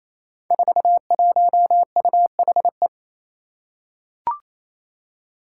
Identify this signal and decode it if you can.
Morse code, heard as a keyed tone, about 28 wpm, 700 Hz. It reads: 41UHE
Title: Five Challenging Letter-Number Combinations 28wpm